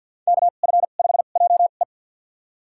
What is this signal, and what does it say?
Morse code, heard as a keyed tone, about 50 wpm, spaced as for 36 wpm, 700 Hz. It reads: KF5JE
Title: USA Call Signs - 5 Characters - 2x Character Spacing - 50wpm